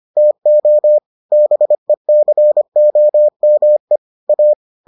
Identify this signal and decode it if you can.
Morse code, heard as a keyed tone, about 25 wpm, 600 Hz. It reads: TO BECOME A